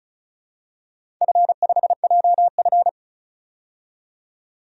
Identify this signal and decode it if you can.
Morse code, heard as a keyed tone, about 34 wpm, 700 Hz. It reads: F5JF
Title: DXCC-100 Call Signs - 4 Characters - 35wpm